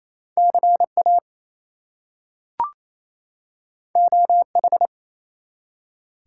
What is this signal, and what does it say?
Morse code, heard as a keyed tone, about 27 wpm, 700 Hz. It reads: CA OH
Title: US States and Canadian Provinces 28wpm